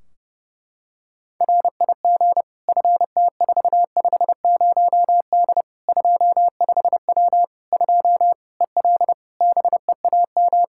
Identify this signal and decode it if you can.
Morse code, heard as a keyed tone, about 30 wpm, 700 Hz. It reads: RIG FT450D 25W 2 EL BEAM
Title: Practice QSO - Exchange 2 - Info - 30wpm